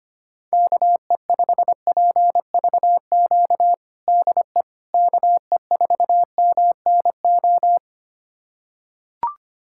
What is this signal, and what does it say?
Morse code, heard as a keyed tone, about 25 wpm, 700 Hz. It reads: KE5PVQ DE KE4MNO
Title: USA Callsign DE Callsign 25wpm